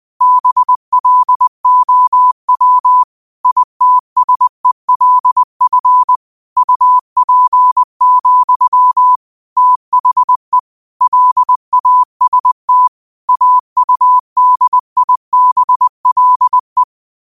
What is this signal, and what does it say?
Morse code, heard as a keyed tone, about 20 wpm, 1000 Hz. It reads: BLOW ITSELF UP, THE LAST AUDIBLE